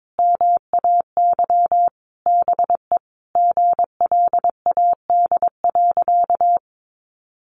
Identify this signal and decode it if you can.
Morse code, heard as a keyed tone, about 22 wpm, 700 Hz. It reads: MAY BE GLAD.
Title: Sentences from Top 800 Words 22wpm